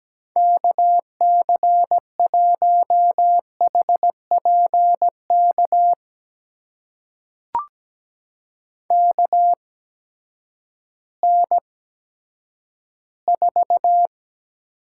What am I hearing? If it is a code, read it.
Morse code, heard as a keyed tone, about 17 wpm, spaced as for 10 wpm, 700 Hz. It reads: KC1HPK KN4
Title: USA Call Signs - 6 Characters - 8x Character Spacing - 17wpm